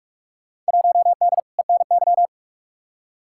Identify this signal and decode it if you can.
Morse code, heard as a keyed tone, about 43 wpm, 700 Hz. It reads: 1D ENY
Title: Field Day Exchanges - 45wpm